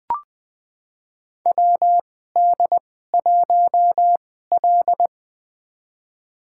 Morse code, heard as a keyed tone, about 21 wpm, spaced as for 15 wpm, 700 Hz. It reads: WD1L
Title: USA Call Signs - 4 Characters - 2x Character Spacing - 20wpm